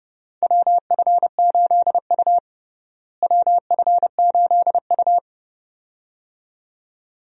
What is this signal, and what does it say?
Morse code, heard as a keyed tone, about 32 wpm, 700 Hz. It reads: WF8U WF8U
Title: USA Call Signs - 4 Characters - Repeated 2x - 30wpm